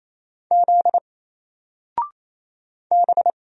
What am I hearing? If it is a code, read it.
Morse code, heard as a keyed tone, about 27 wpm, 700 Hz. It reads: Z B
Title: Single Letter-Number 28wpm